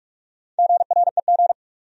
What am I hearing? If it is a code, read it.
Morse code, heard as a keyed tone, about 47 wpm, 700 Hz. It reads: GREG
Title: USA Common Names - Max 4 Letters - 45wpm